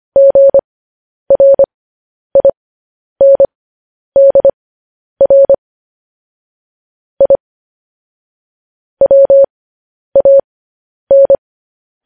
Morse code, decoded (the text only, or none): GRINDR I WAN